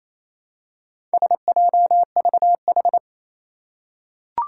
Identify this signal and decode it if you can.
Morse code, heard as a keyed tone, about 29 wpm, 700 Hz. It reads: SJVH
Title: Four Challenging Letter-Number Combinations 28wpm